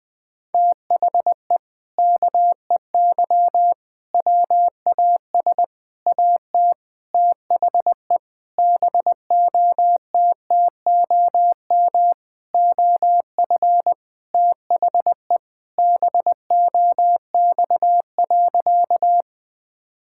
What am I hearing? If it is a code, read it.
Morse code, heard as a keyed tone, about 20 wpm, 700 Hz. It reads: THE KEY WAS AT THE BOTTOM OF THE BOX.
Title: Sentences with 4-Word Phrases - 20wpm